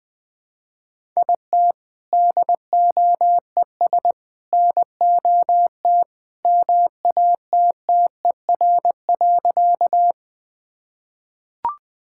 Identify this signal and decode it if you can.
Morse code, heard as a keyed tone, about 20 wpm, 700 Hz. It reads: IT DOES NOT MATTER.
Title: Sentences with 3-Word Phrases - 20wpm